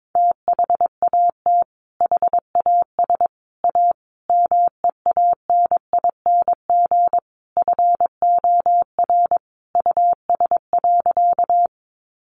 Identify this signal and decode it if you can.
Morse code, heard as a keyed tone, about 22 wpm, 700 Hz. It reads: THAT HAS A MEANING FOR US.
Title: Sentences from Top 900 Words 22wpm